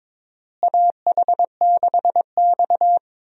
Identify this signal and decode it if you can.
Morse code, heard as a keyed tone, about 22 wpm, 700 Hz. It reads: AH6X
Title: USA Call Signs - 4 Characters - Repeated 2x - 22wpm